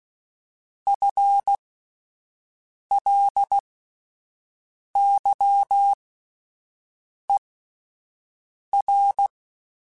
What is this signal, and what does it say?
Morse code, heard as a keyed tone, about 16 wpm, spaced as for 5 wpm, 800 Hz. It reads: FLYER
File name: Spaced / FL Runs sp mp3s / FL Runs sp @10